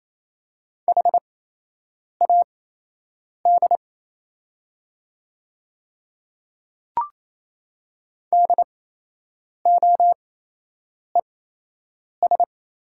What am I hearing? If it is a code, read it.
Morse code, heard as a keyed tone, about 28 wpm, spaced as for 8 wpm, 690 Hz. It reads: HAD DOES
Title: 8x Character Spacing - Top 200 Words - 28wpm